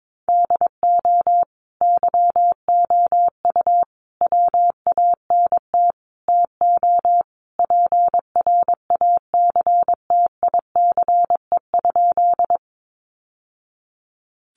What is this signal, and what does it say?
Morse code, heard as a keyed tone, about 22 wpm, 700 Hz. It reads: DO YOU WANT TO PRACTICE?
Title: Sentences from Top 700 Words 22wpm